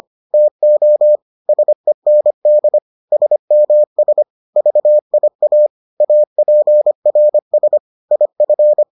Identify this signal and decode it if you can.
Morse code, heard as a keyed tone, about 25 wpm, 600 Hz. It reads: TO SEND SMS VIA APRS IF